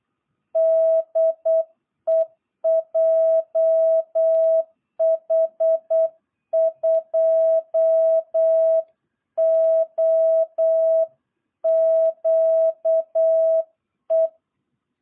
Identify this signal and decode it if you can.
Morse code, heard as a keyed tone, about 8 wpm, 660 Hz. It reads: DE JH2OQE
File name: jh8cbh / 2022cw